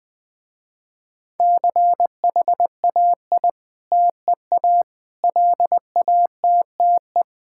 Morse code, heard as a keyed tone, about 20 wpm, 700 Hz. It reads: CHAI TEA LATTE
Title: Difficult Words with Hints - 20wpm